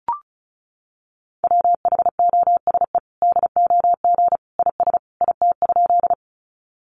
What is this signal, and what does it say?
Morse code, heard as a keyed tone, about 34 wpm, 700 Hz. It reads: WHOSE DOG IS IT?
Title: Sentences from Top 1200 Words 35wpm